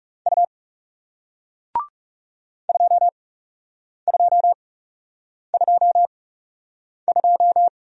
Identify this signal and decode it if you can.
Morse code, heard as a keyed tone, about 41 wpm, spaced as for 8 wpm, 700 Hz. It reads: U 2222